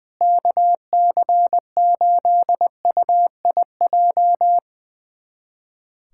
Morse code, heard as a keyed tone, about 20 wpm, 700 Hz. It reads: KC8UIJ
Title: USA Callsign DE Callsign - 6 Characters - 20wpm